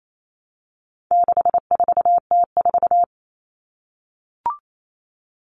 Morse code, heard as a keyed tone, about 28 wpm, 700 Hz. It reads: 64T4
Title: All Cut Numbers - 4-Digit 28wpm